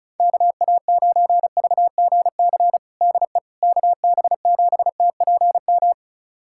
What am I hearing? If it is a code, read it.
Morse code, heard as a keyed tone, about 36 wpm, 700 Hz. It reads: KA9VGC DE KB7TPM
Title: USA Callsign DE Callsign - 6 Characters - 35wpm